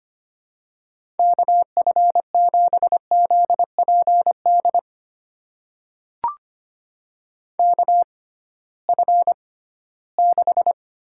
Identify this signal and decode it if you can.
Morse code, heard as a keyed tone, about 25 wpm, spaced as for 16 wpm, 700 Hz. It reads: KF7ZPD KF6
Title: USA Call Signs - 6 Characters - 6x Character Spacing - 25wpm